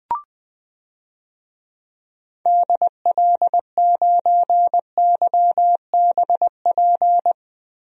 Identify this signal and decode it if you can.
Morse code, heard as a keyed tone, about 20 wpm, 700 Hz. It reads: DL9YBP
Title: DXCC-100 Call Signs - 6 Characters - Repeated 2x - 20wpm